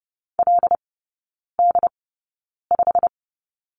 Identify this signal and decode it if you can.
Morse code, heard as a keyed tone, about 30 wpm, spaced as for 9 wpm, 700 Hz. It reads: LD5
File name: course / 30:10wpm / 087 Lesson 15e - Number 5 - Three Characters 30wpm